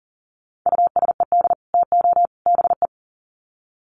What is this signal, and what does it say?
Morse code, heard as a keyed tone, about 40 wpm, 700 Hz. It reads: USED TO BE
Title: Sets of 3 Words 40wpm